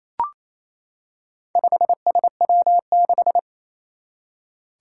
Morse code, heard as a keyed tone, about 27 wpm, 700 Hz. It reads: 5SW6
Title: Four Challenging Letter-Number Combinations 28wpm